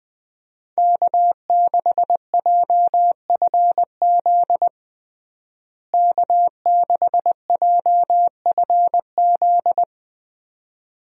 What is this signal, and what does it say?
Morse code, heard as a keyed tone, about 20 wpm, 700 Hz. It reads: K6JFZ K6JFZ
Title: USA Call Signs - 5 Characters - Repeated 2x - 20wpm